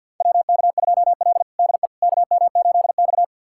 Morse code, heard as a keyed tone, about 46 wpm, 690 Hz. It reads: WK2L DE KM8X
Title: USA Callsign DE Callsign - 4 Characters - 50wpm